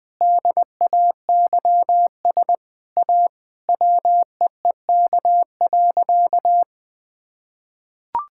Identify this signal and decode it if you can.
Morse code, heard as a keyed tone, about 20 wpm, 700 Hz. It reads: DAYS A WEEK.
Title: Sentences with 7-Word Phrases - 20wpm